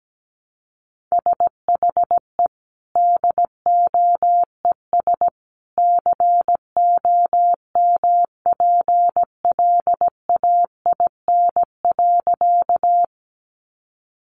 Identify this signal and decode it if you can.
Morse code, heard as a keyed tone, about 17 wpm, 700 Hz. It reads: SHE DOES COMPLAIN.
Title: Sentences from Top 1300 Words 17wpm